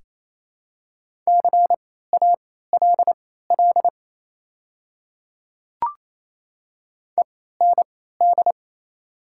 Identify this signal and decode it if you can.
Morse code, heard as a keyed tone, about 28 wpm, spaced as for 16 wpm, 690 Hz. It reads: CALL END
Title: 3x Character Spacing - Top 200 Words - 28wpm